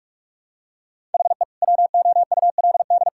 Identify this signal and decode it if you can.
Morse code, heard as a keyed tone, about 46 wpm, 700 Hz. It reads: HE WOULD
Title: Sentences from Top 1000 Words - Review Part 5 - 45wpm